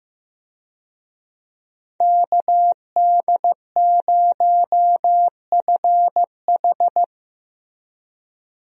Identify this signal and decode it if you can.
Morse code, heard as a keyed tone, about 15 wpm, 700 Hz. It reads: KD0FH
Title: USA Call Signs - 5 Characters - Repeated 2x - 15wpm